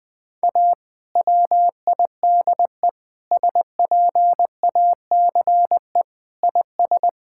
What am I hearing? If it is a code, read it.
Morse code, heard as a keyed tone, about 20 wpm, 700 Hz. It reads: A WIDE SPACE IS
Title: Sentences from Top 1000 Words - Review Part 6 - 20wpm